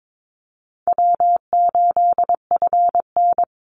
Morse code, heard as a keyed tone, about 22 wpm, 700 Hz. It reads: W8FN
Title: Easy Call Signs Repeated 3x - 22wpm